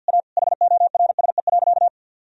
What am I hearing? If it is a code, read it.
Morse code, heard as a keyed tone, about 46 wpm, 700 Hz. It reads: A HORSE.
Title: Sentences from Top 1000 Words - Review Part 2 - 50wpm